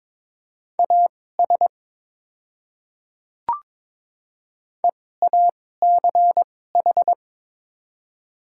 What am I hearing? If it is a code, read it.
Morse code, heard as a keyed tone, about 22 wpm, spaced as for 16 wpm, 700 Hz. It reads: AS EACH
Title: 2x Character Spacing - Top 200 Words - 22wpm